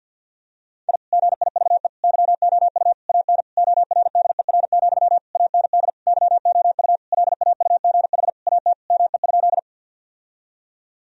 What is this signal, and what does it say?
Morse code, heard as a keyed tone, about 47 wpm, 700 Hz. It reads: I GIVE YOU AN ORDER, AND YOU LAUGH AT ME?
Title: Sentences from Top 1000 Words - Review Part 2 - 50wpm